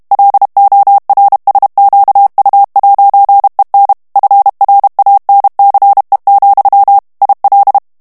Morse code, heard as a keyed tone, about 32 wpm, 800 Hz. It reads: LORSQU'EN FRANCE, IL
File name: morse mp3 / FRANCAIS TEXTE CLAIR 1 VITESSE 30 MOTS MINUTES